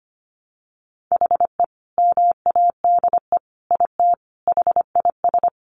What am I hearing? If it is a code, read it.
Morse code, heard as a keyed tone, about 25 wpm, 700 Hz. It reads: HE MADE IT HIS
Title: Sentences from Top 1400 Words 25wpm